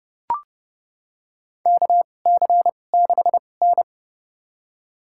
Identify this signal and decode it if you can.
Morse code, heard as a keyed tone, about 32 wpm, spaced as for 22 wpm, 700 Hz. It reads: KC6N
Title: USA Call Signs - 2x Character Spacing - 30wpm